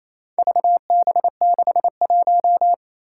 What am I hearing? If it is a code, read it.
Morse code, heard as a keyed tone, about 27 wpm, 700 Hz. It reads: VB61
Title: Four Challenging Letter-Number Combinations 28wpm